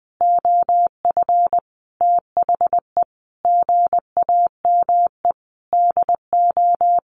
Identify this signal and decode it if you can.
Morse code, heard as a keyed tone, about 20 wpm, 700 Hz. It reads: OF THE GAME DO
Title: Sentences from Top 1400 Words 20wpm